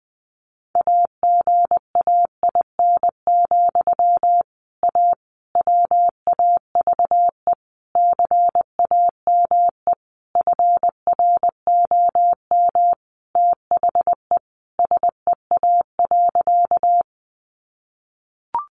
Morse code, heard as a keyed tone, about 20 wpm, 700 Hz. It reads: AGAIN, A WAVE CAME FROM THE SEA.
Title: Sentences from Top 1400 Words 20wpm